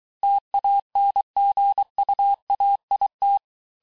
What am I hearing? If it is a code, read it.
Morse code, heard as a keyed tone, about 23 wpm, 800 Hz. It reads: TANGUAIT
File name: cours-du-mercredi / audios / COURS-855--texte-clair-vitesse-22-mpm